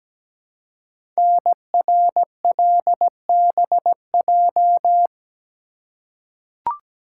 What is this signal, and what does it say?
Morse code, heard as a keyed tone, about 17 wpm, 700 Hz. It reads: NRLBJ
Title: Code Groups - 5 Letters - 17wpm